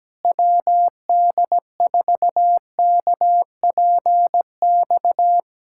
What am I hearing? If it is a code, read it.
Morse code, heard as a keyed tone, about 17 wpm, 700 Hz. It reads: WD4KPX
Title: USA Callsign DE Callsign 17wpm